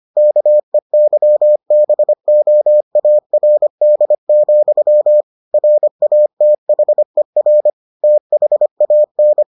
Morse code, heard as a keyed tone, about 25 wpm, 600 Hz. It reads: KEYBOARD, RATHER THAN